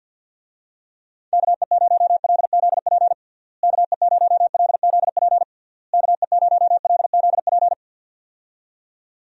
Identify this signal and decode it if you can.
Morse code, heard as a keyed tone, about 48 wpm, 700 Hz. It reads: KE0LZP KE0LZP KE0LZP
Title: USA Call Signs - 6 Characters - Repeated 3x - 50wpm